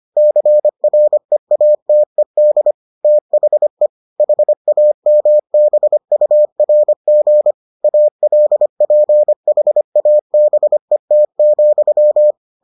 Morse code, heard as a keyed tone, about 25 wpm, 600 Hz. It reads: CREATED THE HAMBURG ALPHABET,